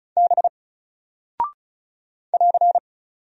Morse code, heard as a keyed tone, about 36 wpm, 700 Hz. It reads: B <AR>
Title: Single Character (Letter:Number:Punctuation:Prosign) 35wpm